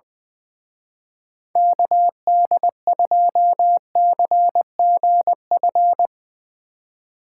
Morse code, heard as a keyed tone, about 22 wpm, 700 Hz. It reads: KD2CGF
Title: Speed Racing - USA 2x3 Call Signs 20wpm